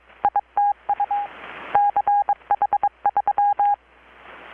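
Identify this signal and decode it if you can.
Morse code, heard as a keyed tone, about 22 wpm, 820 Hz. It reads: ITU CH3
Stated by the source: Morse from KPH's Night of Nights XX maritime radiotelegraphy event, received in Chiapas, Mexico about 3,800 km away on a Kenwood R-600 with a grounded 10 m long-wire antenna.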